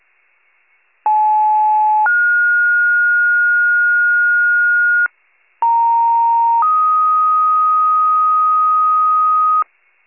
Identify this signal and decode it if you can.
Two-tone page, heard as a one-second tone, then a three-second tone, about 860 then 1430 Hz, about 930 then 1250 Hz, with noise. It is heard from 1.0 to 9.6 s.
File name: pager / 75 FIRE / wcalert